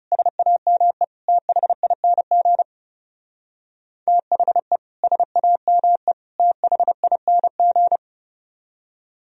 Morse code, heard as a keyed tone, about 40 wpm, 700 Hz. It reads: SAME THING THE SAME THING